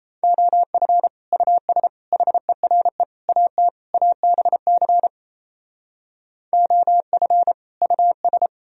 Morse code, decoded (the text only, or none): OF US HERE AT ABC OF US